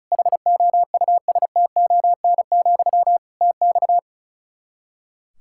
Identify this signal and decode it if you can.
Morse code, heard as a keyed tone, about 34 wpm, 700 Hz. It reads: HOUSTON, TX
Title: K1USN SST QSO Practice 35wpm